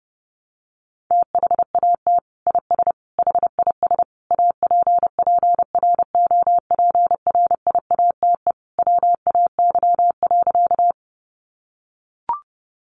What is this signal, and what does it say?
Morse code, heard as a keyed tone, about 30 wpm, 700 Hz. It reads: THAT IS HIS APPROPRIATE WAY.
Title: Sentences from Top 900 Words 30wpm